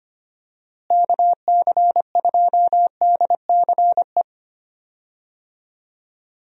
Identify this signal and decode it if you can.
Morse code, heard as a keyed tone, about 25 wpm, 700 Hz. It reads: KC2DCE
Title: USA Call Signs - 6 Characters - Repeated 3x - 25wpm